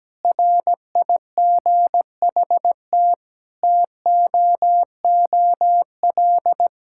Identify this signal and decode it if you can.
Morse code, heard as a keyed tone, about 17 wpm, 700 Hz. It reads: RIGHT TOOL
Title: Sentences from Top 1000 Words - Review Part 6 - 17wpm